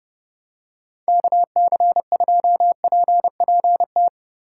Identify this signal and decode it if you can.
Morse code, heard as a keyed tone, about 30 wpm, 700 Hz. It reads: KC2PPT
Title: USA Call Signs - 6 Characters - 6x Character Spacing - 30wpm